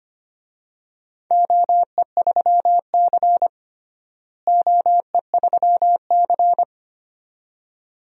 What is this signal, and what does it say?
Morse code, heard as a keyed tone, about 25 wpm, 700 Hz. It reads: OE3C OE3C
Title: DXCC-100 Call Signs - 4 Characters - Repeated 2x - 25wpm